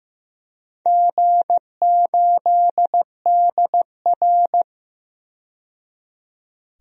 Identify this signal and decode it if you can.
Morse code, heard as a keyed tone, about 15 wpm, 700 Hz. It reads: G8DR
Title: DXCC-100 Call Signs - 4 Characters - 15wpm